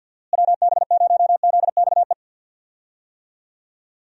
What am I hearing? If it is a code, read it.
Morse code, heard as a keyed tone, about 51 wpm, 700 Hz. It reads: WB0ZXE
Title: USA Call Signs - 6 Characters - Repeated 3x - 50wpm